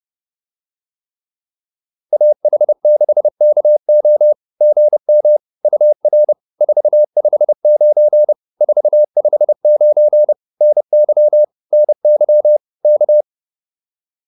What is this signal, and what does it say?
Morse code, heard as a keyed tone, about 30 wpm, 600 Hz. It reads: AH6KO GM UR 459 459 NY NY K
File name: POTA208 30